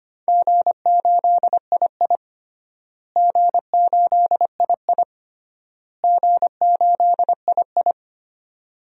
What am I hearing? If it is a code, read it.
Morse code, heard as a keyed tone, about 25 wpm, 700 Hz. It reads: G8II G8II G8II
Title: DXCC-100 Call Signs - 4 Characters - Repeated 3x - 25wpm